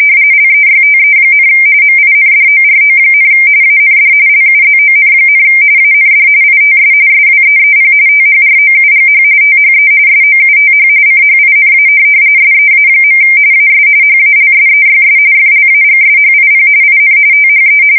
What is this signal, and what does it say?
RTTY, about 45.45 baud. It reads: LEYTE, MINDORO, AND LUZON, AND ITS SWEEP THROUGH THE SOUTH CHINA SEA. AS HALSEYS FLAGSHIP IN THE
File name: fsk170 / USS-New-Jersey